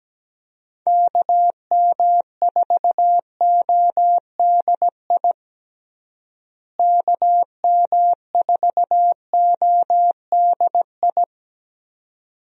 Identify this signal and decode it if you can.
Morse code, heard as a keyed tone, about 17 wpm, 700 Hz. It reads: KM4ODI KM4ODI
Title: USA Call Signs - 6 Characters - Repeated 2x - 17wpm